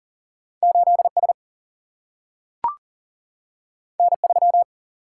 Morse code, heard as a keyed tone, about 39 wpm, 700 Hz. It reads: 8S N3
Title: Single-Letter Single-Number 40wpm